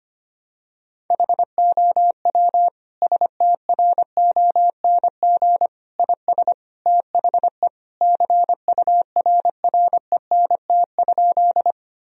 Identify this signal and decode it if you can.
Morse code, heard as a keyed tone, about 25 wpm, 700 Hz. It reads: HOW STRONG IS THE CURRENT?
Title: Sentences from Top 1000 Words - Review Part 4 - 25wpm